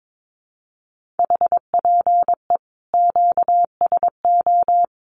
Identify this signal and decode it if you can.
Morse code, heard as a keyed tone, about 22 wpm, 700 Hz. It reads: HPE QSO
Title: Common QSO Phrases 22wpm